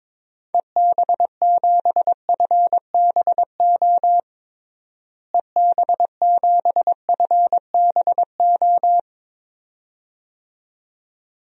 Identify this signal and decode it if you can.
Morse code, heard as a keyed tone, about 21 wpm, 700 Hz. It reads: EB7FBO EB7FBO
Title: DXCC-100 Call Signs - 6 Characters - Repeated 2x - 22wpm